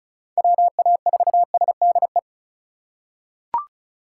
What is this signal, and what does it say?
Morse code, heard as a keyed tone, about 34 wpm, 690 Hz. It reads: WA4SDE
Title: USA Call Signs - 6 Characters - 2x Character Spacing - 35wpm